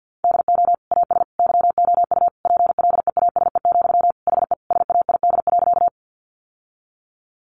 Morse code, heard as a keyed tone, about 50 wpm, 700 Hz. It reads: DO AS YOU PLEASE, HE SAID.
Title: Sentences from Top 1500 Words 50wpm